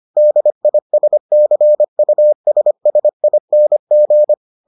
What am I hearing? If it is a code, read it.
Morse code, heard as a keyed tone, about 25 wpm, 600 Hz. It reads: DISCUSSING